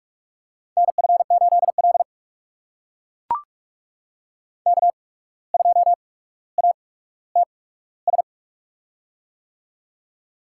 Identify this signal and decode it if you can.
Morse code, heard as a keyed tone, about 44 wpm, spaced as for 18 wpm, 700 Hz. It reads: NF8L K2ATS
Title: USA Call Signs - 8x Character Spacing - 45wpm